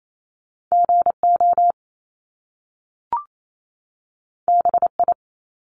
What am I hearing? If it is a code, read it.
Morse code, heard as a keyed tone, about 28 wpm, 700 Hz. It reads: GO BI
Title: Top Two Letter Combinations in Words 28wpm